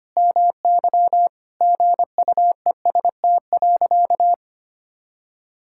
Morse code, heard as a keyed tone, about 25 wpm, 700 Hz. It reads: MY GUEST.
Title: Sentences from Top 1600 Words 25wpm